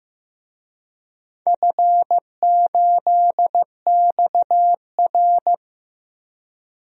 Morse code, heard as a keyed tone, about 15 wpm, 700 Hz. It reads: F8XR
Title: DXCC-100 Call Signs - 4 Characters - Repeated 2x - 15wpm